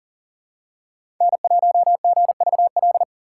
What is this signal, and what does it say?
Morse code, heard as a keyed tone, about 40 wpm, 690 Hz. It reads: N1GVL
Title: USA Call Signs - 3x Character Spacing - 40wpm